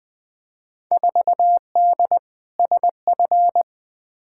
Morse code, heard as a keyed tone, about 20 wpm, 700 Hz. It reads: 4D SF
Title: Field Day Exchanges Encore - 20wpm